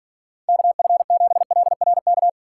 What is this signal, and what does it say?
Morse code, heard as a keyed tone, about 50 wpm, 700 Hz. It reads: KF7LRK
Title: Speed Racing - USA 2x3 Call Signs 40wpm